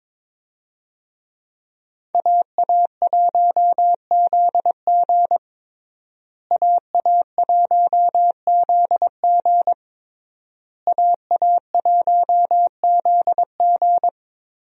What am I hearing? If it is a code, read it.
Morse code, heard as a keyed tone, about 22 wpm, 700 Hz. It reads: AA1ZG AA1ZG AA1ZG
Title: USA Call Signs - 5 Characters - Repeated 3x - 22wpm